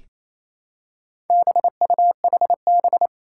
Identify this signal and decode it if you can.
Morse code, heard as a keyed tone, about 27 wpm, 700 Hz. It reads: BUHB
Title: Four Challenging Letter-Number Combinations 28wpm